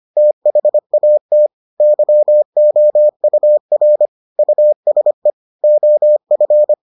Morse code, heard as a keyed tone, about 25 wpm, 600 Hz. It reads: THAT YOUR USE OF